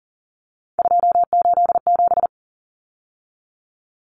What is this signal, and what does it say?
Morse code, heard as a keyed tone, about 40 wpm, 700 Hz. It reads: 287
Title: Numbers - 3-Digit 40wpm